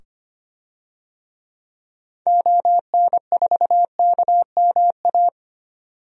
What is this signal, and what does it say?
Morse code, heard as a keyed tone, about 25 wpm, 700 Hz. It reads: ON4KMA
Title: DXCC-100 Call Signs - 6 Characters - 25wpm